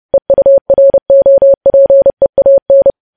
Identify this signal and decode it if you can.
Morse code, heard as a keyed tone, about 30 wpm, 550 Hz. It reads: EUROPEAN